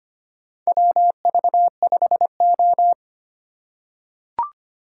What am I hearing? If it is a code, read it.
Morse code, heard as a keyed tone, about 25 wpm, 700 Hz. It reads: WV5O
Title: USA Call Signs - 4 Characters - 6x Character Spacing - 25wpm